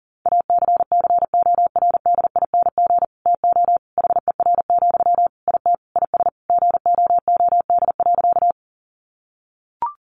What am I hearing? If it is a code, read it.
Morse code, heard as a keyed tone, about 40 wpm, 700 Hz. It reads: ACCORDING TO HER, IT IS GOOD.
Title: Sentences from Top 800 Words 40wpm